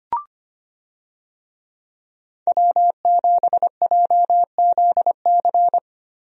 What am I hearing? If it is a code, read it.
Morse code, heard as a keyed tone, about 25 wpm, 700 Hz. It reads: W7JZC
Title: USA Call Signs - 5 Characters - Repeated 2x - 25wpm